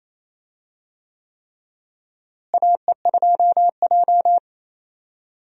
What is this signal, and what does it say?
Morse code, heard as a keyed tone, about 28 wpm, 700 Hz. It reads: AE2J
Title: USA Call Signs - 4 Characters - Repeated 3x - 28wpm